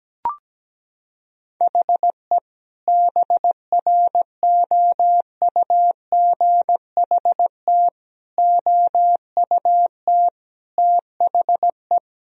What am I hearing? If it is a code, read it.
Morse code, heard as a keyed tone, about 17 wpm, 700 Hz. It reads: HE BROUGHT OUT THE
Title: Sentences from Top 1000 Words - Review Part 3 - 17wpm